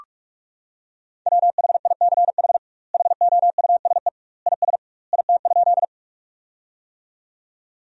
Morse code, heard as a keyed tone, about 46 wpm, 700 Hz. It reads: WHICH HOUSE IS IT?
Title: 2x Word-Spacing - Sentences from Top 200 Words - 45wpm